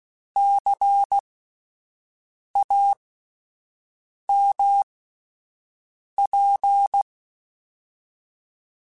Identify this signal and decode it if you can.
Morse code, heard as a keyed tone, about 16 wpm, spaced as for 5 wpm, 800 Hz. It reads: CAMP